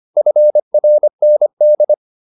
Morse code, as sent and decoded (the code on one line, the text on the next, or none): ..-. .-. -. -..
FRND